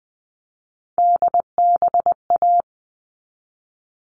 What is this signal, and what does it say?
Morse code, heard as a keyed tone, about 20 wpm, 700 Hz. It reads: DBA
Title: All Cut Numbers - 3-Digit 20wpm